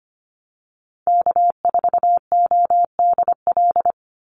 Morse code, heard as a keyed tone, about 25 wpm, 700 Hz. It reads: K4ODL
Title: Call Signs - 25wpm